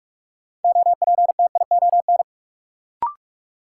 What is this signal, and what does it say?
Morse code, heard as a keyed tone, about 46 wpm, 700 Hz. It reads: OPTION
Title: Sentences from Top 1300 Words - New Word Drill Alt - 45wpm